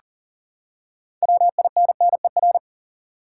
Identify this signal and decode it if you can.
Morse code, heard as a keyed tone, about 40 wpm, 700 Hz. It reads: WINNER
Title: Six Letter Words - 40wpm